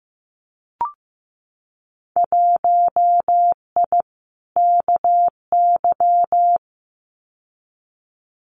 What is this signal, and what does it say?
Morse code, heard as a keyed tone, about 15 wpm, 700 Hz. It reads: 1I KY
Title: Winter Field Day Exchanges 15wpm